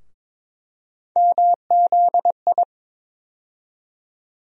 Morse code, heard as a keyed tone, about 22 wpm, 700 Hz. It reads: MZI
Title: USA Call Sign Postfixes - 3 Letters 22wpm